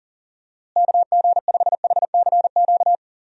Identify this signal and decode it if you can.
Morse code, heard as a keyed tone, about 40 wpm, 700 Hz. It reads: KG5HCQ
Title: USA Call Signs - 6 Characters - Repeated 2x - 40wpm